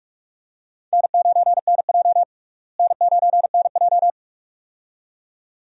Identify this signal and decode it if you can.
Morse code, heard as a keyed tone, about 43 wpm, 700 Hz. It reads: N9NJ N9NJ
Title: USA Call Signs - 4 Characters - Repeated 2x - 45wpm